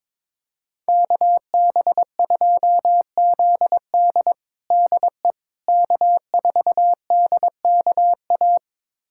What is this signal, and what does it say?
Morse code, heard as a keyed tone, about 22 wpm, 700 Hz. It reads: KB2ZD DE K4DKA
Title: USA Callsign DE Callsign - 5 Characters - 22wpm